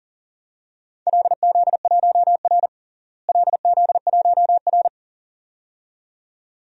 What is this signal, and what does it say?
Morse code, heard as a keyed tone, about 40 wpm, 700 Hz. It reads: LZ1R LZ1R
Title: DXCC-100 Call Signs - 4 Characters - Repeated 2x - 40wpm